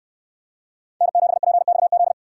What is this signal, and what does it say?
Morse code, heard as a keyed tone, about 68 wpm, 700 Hz. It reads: N7PXZ